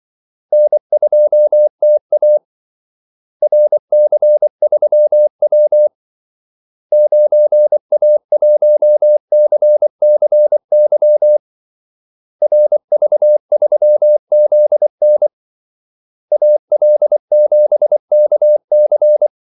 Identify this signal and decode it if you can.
Morse code, heard as a keyed tone, about 24 wpm, 600 Hz. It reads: N2TA RC3W 9A1CCY RV3ZN AL7KC
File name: Callsigns-24wpm